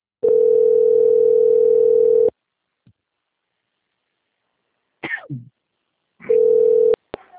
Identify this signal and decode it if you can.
Ringback tone (North America), 440 + 480 Hz.
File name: out- - - - - .57046